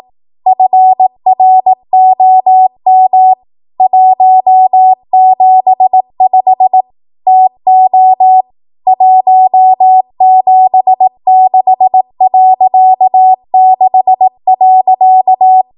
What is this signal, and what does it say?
Morse code, heard as a keyed tone, about 18 wpm, 750 Hz. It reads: FROM 175 TO 176.6.
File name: Archive / 18 WPM / 18WPM